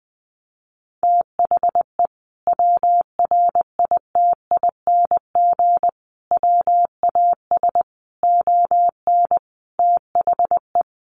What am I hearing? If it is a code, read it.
Morse code, heard as a keyed tone, about 20 wpm, 700 Hz. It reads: THE WRITING WAS ON THE
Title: Sentences from Top 1100 Words 20wpm